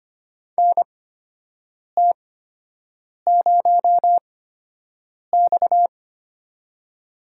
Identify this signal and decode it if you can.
Morse code, heard as a keyed tone, about 25 wpm, spaced as for 7 wpm, 700 Hz. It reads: NT0X
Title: USA Call Signs - 4 Characters - 8x Character Spacing - 25wpm